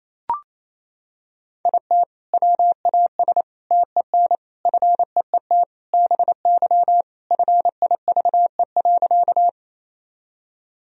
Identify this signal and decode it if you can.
Morse code, heard as a keyed tone, about 28 wpm, 700 Hz. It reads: IT WAS TEN FEET BY FIVE.
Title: Sentences from Top 1000 Words - Review Part 2 - 28wpm